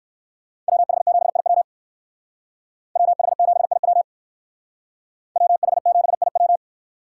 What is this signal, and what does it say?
Morse code, heard as a keyed tone, about 58 wpm, 700 Hz. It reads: WH7IW WH7IW WH7IW